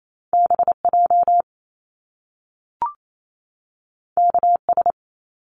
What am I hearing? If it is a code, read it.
Morse code, heard as a keyed tone, about 28 wpm, 700 Hz. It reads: BJ KS
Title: Top Two Letter Combinations in Words 28wpm